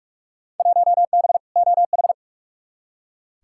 Morse code, heard as a keyed tone, about 44 wpm, 700 Hz. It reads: 1B OH
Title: Field Day Exchanges - 45wpm